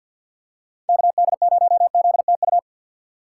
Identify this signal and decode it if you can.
Morse code, heard as a keyed tone, about 48 wpm, 700 Hz. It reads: KD0ZTU
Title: USA Call Signs - 6x Character Spacing - 50wpm